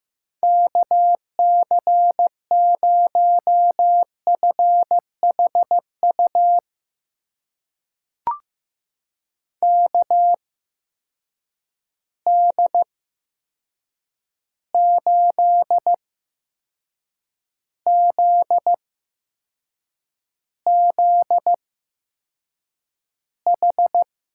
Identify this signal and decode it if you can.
Morse code, heard as a keyed tone, about 15 wpm, spaced as for 6 wpm, 700 Hz. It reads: KC0FHU KD8ZZH
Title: USA Call Signs - 6 Characters - 8x Character Spacing - 15wpm